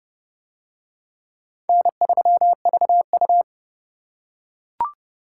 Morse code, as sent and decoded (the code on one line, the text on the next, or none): -. ...-- ...- ..-
N3VU